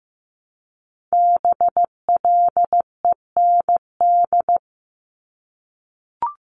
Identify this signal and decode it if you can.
Morse code, heard as a keyed tone, about 15 wpm, 700 Hz. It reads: BLEND
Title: Multi-letter Phonemes 15wpm